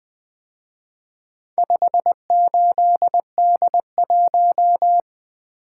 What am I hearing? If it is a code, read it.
Morse code, heard as a keyed tone, about 22 wpm, 700 Hz. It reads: 58D1
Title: Speed-Racing - Four Letter-Number Combinations - 20wpm